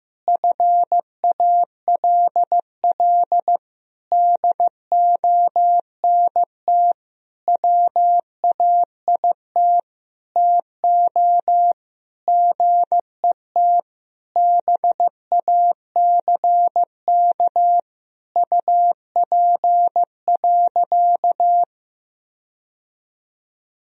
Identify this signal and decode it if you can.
Morse code, heard as a keyed tone, about 15 wpm, 700 Hz. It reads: FALL DONT WAIT TO GET BACK UP.
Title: Sentences from Top 1000 Words - Review Part 2 - 15wpm